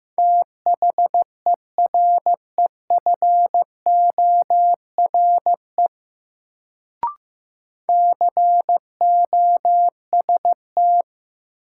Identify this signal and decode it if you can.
Morse code, heard as a keyed tone, about 15 wpm, 700 Hz. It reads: THEREFORE COST
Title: Sentences from Top 800 Words - New Word Drill - 15wpm